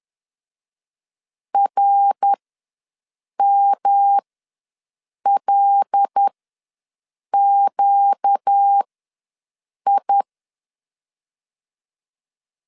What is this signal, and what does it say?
Morse code, heard as a keyed tone, about 10 wpm, spaced as for 6 wpm, 800 Hz. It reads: RMLQI
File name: LEZIONI / 5 LBVC / 5 L ws 10wpm 3636